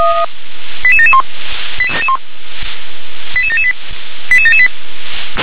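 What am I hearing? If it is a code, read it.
Five-tone selective call (ZVEI), heard as quick runs of five tones, tones of about 70 ms. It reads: 88771 88778 88778
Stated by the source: ZVEI five-tone call 88954 received on SDR.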